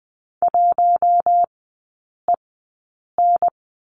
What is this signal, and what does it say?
Morse code, heard as a keyed tone, about 20 wpm, spaced as for 8 wpm, 700 Hz. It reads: 1EN
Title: 041 Lesson 8e - Number 1 - Three Characters 20wpm